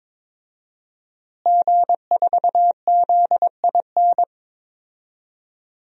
Morse code, heard as a keyed tone, about 22 wpm, 700 Hz. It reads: G4ZIN
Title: DXCC-100 Call Signs - 5 Characters - Repeated 2x - 22wpm